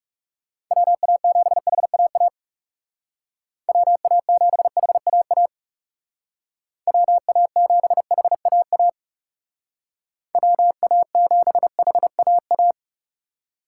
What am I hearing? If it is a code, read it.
Morse code, heard as a keyed tone, about 37 wpm, 700 Hz. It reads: WA7HAA WA7HAA WA7HAA WA7HAA
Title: Speed Racing - USA 2x3 Call Signs 30wpm